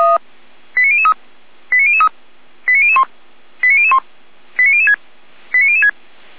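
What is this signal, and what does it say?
Five-tone selective call (ZVEI), heard as quick runs of five tones, tones of about 70 ms. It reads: 89003 89003 89001 89001 89006 89006
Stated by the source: Five-tone ZVEI selective call 89099, received on an SDR.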